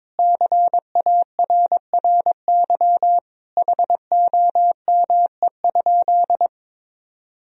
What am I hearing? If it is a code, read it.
Morse code, heard as a keyed tone, about 21 wpm, 700 Hz. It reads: CARRY HOME?
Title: Sentences from Top 1000 Words - Review Part 1 - 22wpm